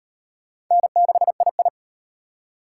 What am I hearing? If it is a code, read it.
Morse code, heard as a keyed tone, about 38 wpm, 700 Hz. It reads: N6II